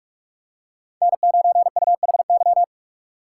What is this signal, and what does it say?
Morse code, heard as a keyed tone, about 44 wpm, 700 Hz. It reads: N9UHY
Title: USA Call Signs - 5 Characters - 4x Character Spacing - 45wpm